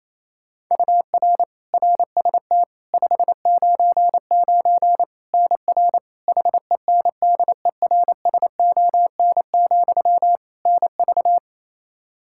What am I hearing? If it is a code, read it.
Morse code, heard as a keyed tone, about 28 wpm, 700 Hz. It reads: UR RST 599 NR HENDERSON, NV
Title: Practice QSO - Exchange 1 - Beginning - 28wpm